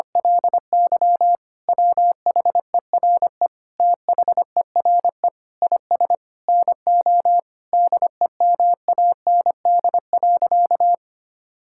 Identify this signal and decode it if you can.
Morse code, heard as a keyed tone, about 25 wpm, 700 Hz. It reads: LY WHERE THERE IS NO DEMAND.
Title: Sentences from Top 1000 Words - Review Part 5 - 25wpm